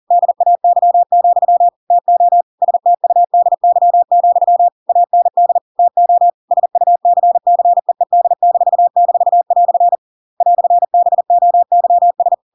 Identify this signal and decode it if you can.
Morse code, heard as a keyed tone, about 40 wpm, 700 Hz. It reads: DAY, TO STUDY, AND TO SUCCEED--" "BOYS